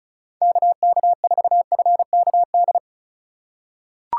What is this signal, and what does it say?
Morse code, heard as a keyed tone, about 36 wpm, 700 Hz. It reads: KK4FKD
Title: USA Call Signs - 4x Character Spacing - 35wpm